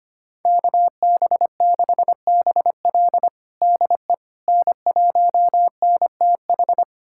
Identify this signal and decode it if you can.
Morse code, heard as a keyed tone, about 25 wpm, 700 Hz. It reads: KB6BL DE N1NTH
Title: USA Callsign DE Callsign - 5 Characters - 25wpm